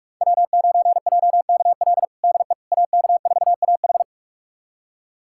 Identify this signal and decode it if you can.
Morse code, heard as a keyed tone, about 42 wpm, 700 Hz. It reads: W9JKL DE AK4AH
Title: USA Callsign DE Callsign 45wpm